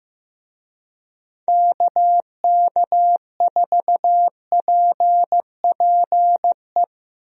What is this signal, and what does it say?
Morse code, heard as a keyed tone, about 15 wpm, 700 Hz. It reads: KK4PPE
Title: USA Call Signs - 6 Characters - Repeated 3x - 15wpm